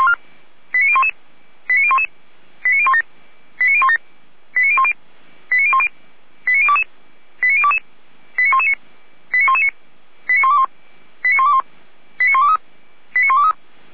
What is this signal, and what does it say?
Five-tone selective call (ZVEI), heard as quick runs of five tones, tones of about 70 ms. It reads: 89011 89011 89017 89017 89010 89010 89022 89022 89119 89119 89121 89121 89123 89123